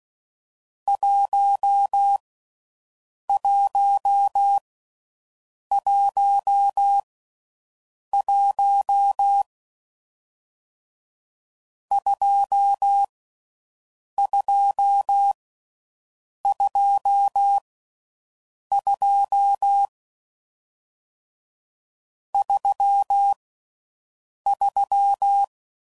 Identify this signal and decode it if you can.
Morse code, heard as a keyed tone, about 16 wpm, 800 Hz. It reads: 1 1 1 1 2 2 2 2 3 3